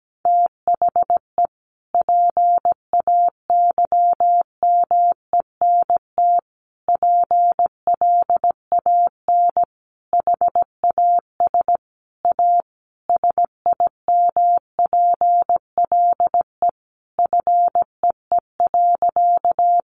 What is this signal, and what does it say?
Morse code, heard as a keyed tone, about 17 wpm, 700 Hz. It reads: THE PAYMENT PLAN HAS A SIMPLE FEE.
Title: Sentences from Top 1500 Words 17wpm